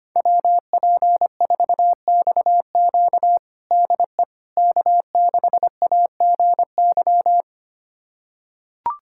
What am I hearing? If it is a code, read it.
Morse code, heard as a keyed tone, about 25 wpm, 700 Hz. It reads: WP4XQ DE K6AGY
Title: USA Callsign DE Callsign 25wpm